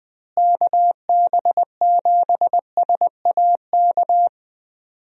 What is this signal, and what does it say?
Morse code, heard as a keyed tone, about 20 wpm, 700 Hz. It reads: KB7SAK
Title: USA Call Signs - 6 Characters - Repeated 2x - 20wpm